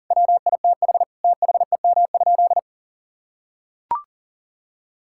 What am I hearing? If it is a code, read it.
Morse code, heard as a keyed tone, about 40 wpm, 700 Hz. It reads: WITH THEM?
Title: Sentences from Top 1000 Words - Review Part 5 - 40wpm